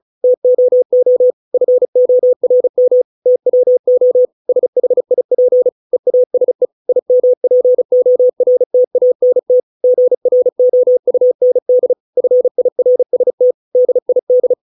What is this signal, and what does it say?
Morse code, heard as a keyed tone, about 34 wpm, 500 Hz. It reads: TOO FORM TWO SHIP EASE IMPORTANT GROUND FIRST DID